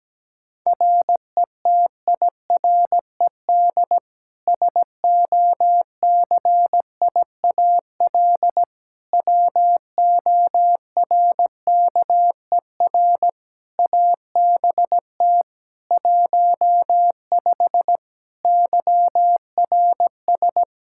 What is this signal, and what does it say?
Morse code, heard as a keyed tone, about 17 wpm, 700 Hz. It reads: RETIRED SOCIAL WORKER ABT 15 YRS
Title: Practice QSO - Exchange 3 - More Info - 17wpm